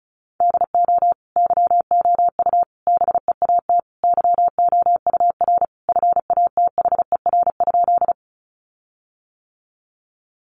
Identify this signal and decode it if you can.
Morse code, heard as a keyed tone, about 34 wpm, 700 Hz. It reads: DO YOU BEAT YOUR FATHER?
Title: Sentences from Top 1300 Words 35wpm